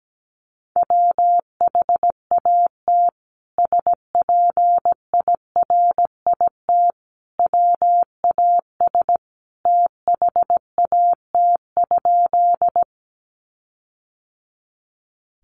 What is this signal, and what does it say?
Morse code, heard as a keyed tone, about 17 wpm, 700 Hz. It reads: WHAT SPIRIT WAS THAT?
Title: Sentences from Top 1400 Words 17wpm